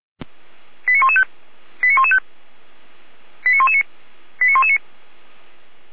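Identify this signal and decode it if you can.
Five-tone selective call (ZVEI), heard as quick runs of five tones, tones of about 70 ms. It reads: 89115 89115 89119 89119
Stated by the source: Received by SDR; ZVEI five-tone selective call.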